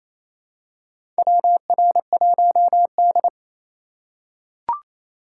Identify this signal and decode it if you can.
Morse code, heard as a keyed tone, about 28 wpm, 700 Hz. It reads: WR1D
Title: USA Call Signs - 4 Characters - 2x Character Spacing - 28wpm